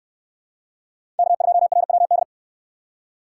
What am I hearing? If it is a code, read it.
Morse code, heard as a keyed tone, about 68 wpm, 700 Hz. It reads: K2RWD